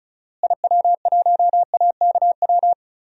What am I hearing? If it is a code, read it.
Morse code, heard as a keyed tone, about 36 wpm, 700 Hz. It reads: IW1AKW
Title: DXCC-100 Call Signs - 6 Characters - Repeated 3x - 35wpm